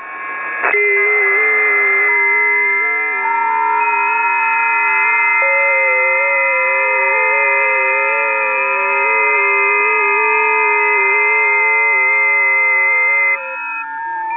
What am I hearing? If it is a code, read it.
FT8.
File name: 20m busy